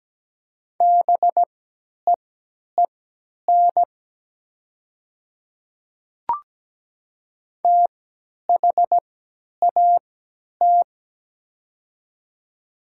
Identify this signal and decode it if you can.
Morse code, heard as a keyed tone, about 17 wpm, spaced as for 10 wpm, 700 Hz. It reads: BEEN THAT
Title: 3x Character Spacing - Top 100 Words - 17wpm